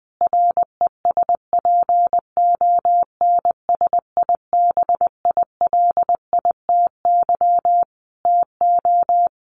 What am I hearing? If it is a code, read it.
Morse code, heard as a keyed tone, about 20 wpm, 700 Hz. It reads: RESPONSIBILITY TO
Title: Sentences from Top 1400 Words 20wpm